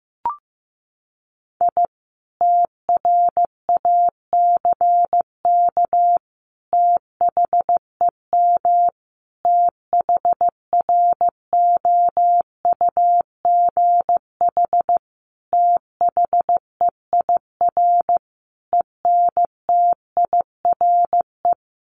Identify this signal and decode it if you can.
Morse code, heard as a keyed tone, about 15 wpm, 700 Hz. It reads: I TRACK THEM THROUGH THEIR ENTIRE
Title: A Sound Of Thunder 15wpm